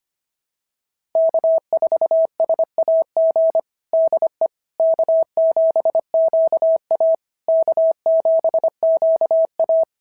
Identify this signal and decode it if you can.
Morse code, heard as a keyed tone, about 25 wpm, 660 Hz. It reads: K4SAG DE K7QA K7QA
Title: SOTA QSO Practice 25wpm